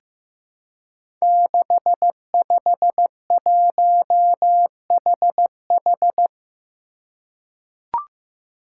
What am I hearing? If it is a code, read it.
Morse code, heard as a keyed tone, about 15 wpm, 700 Hz. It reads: 651HH
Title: Five Challenging Letter-Number Combinations 15wpm